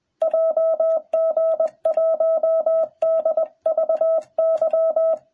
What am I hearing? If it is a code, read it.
Morse code, heard as a keyed tone, about 21 wpm, 660 Hz. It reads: JG1DVY